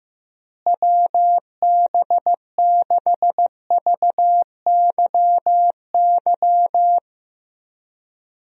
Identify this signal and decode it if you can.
Morse code, heard as a keyed tone, about 15 wpm, 700 Hz. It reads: WB6VYY
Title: USA Call Signs - 6 Characters - Repeated 2x - 15wpm